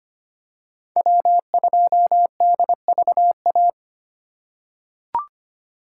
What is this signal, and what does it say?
Morse code, heard as a keyed tone, about 25 wpm, 700 Hz. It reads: W2DVA
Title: USA Call Signs - 5 Characters - 2x Character Spacing - 25wpm